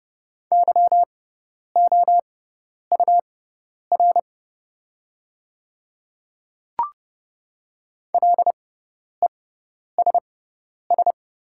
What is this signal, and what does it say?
Morse code, heard as a keyed tone, about 31 wpm, spaced as for 10 wpm, 700 Hz. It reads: YOUR LESS
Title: 6x Character Spacing - Top 500 Words - 30wpm